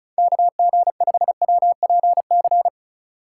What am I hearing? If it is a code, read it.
Morse code, heard as a keyed tone, about 36 wpm, 700 Hz. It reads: KG5WPC
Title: USA Call Signs - 6 Characters - 2x Character Spacing - 35wpm